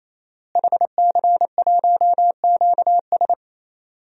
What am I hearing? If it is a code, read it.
Morse code, heard as a keyed tone, about 28 wpm, 700 Hz. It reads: HC1QS
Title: DXCC-100 Call Signs - 5 Characters - Repeated 3x - 28wpm